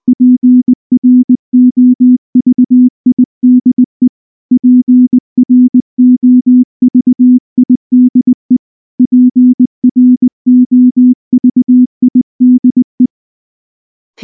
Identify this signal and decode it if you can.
Morse code, heard as a keyed tone, about 19 wpm, 250 Hz. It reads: PROVIDE PROVIDE PROVIDE
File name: morse / morse-training-20WPM-7